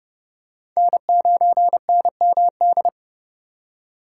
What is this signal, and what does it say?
Morse code, heard as a keyed tone, about 29 wpm, 700 Hz. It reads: N9NMD